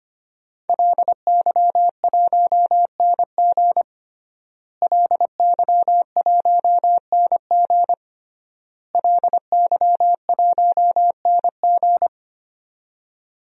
Morse code, heard as a keyed tone, about 25 wpm, 700 Hz. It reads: LY1NG LY1NG LY1NG
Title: DXCC-100 Call Signs - 5 Characters - Repeated 3x - 25wpm